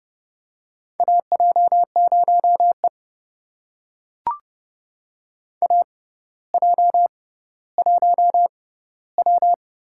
Morse code, heard as a keyed tone, about 31 wpm, spaced as for 16 wpm, 700 Hz. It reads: AJ0E AJ1W
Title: USA Call Signs - 4 Characters - 6x Character Spacing - 30wpm